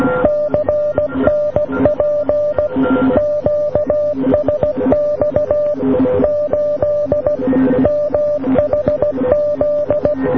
Morse code, heard as a keyed tone, about 16 wpm, 590 Hz. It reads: CNP QSX 8 MHZ